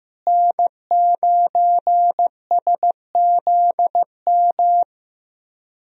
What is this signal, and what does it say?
Morse code, heard as a keyed tone, about 15 wpm, 700 Hz. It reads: N9SZM
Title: USA Call Signs - 5 Characters - Repeated 3x - 15wpm